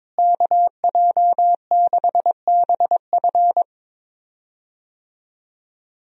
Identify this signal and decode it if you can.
Morse code, heard as a keyed tone, about 22 wpm, 700 Hz. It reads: KJ6BF
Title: USA Call Signs - 5 Characters - Repeated 2x - 22wpm